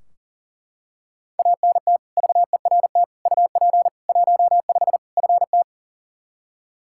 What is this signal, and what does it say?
Morse code, heard as a keyed tone, about 40 wpm, 700 Hz. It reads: ANT VERT UP 15 FT
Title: Practice QSO - Exchange 2 - Info - 40wpm